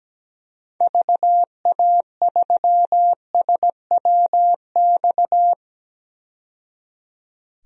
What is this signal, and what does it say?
Morse code, heard as a keyed tone, about 17 wpm, 700 Hz. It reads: VA3SWX
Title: DXCC-100 Call Signs - 6 Characters - 17wpm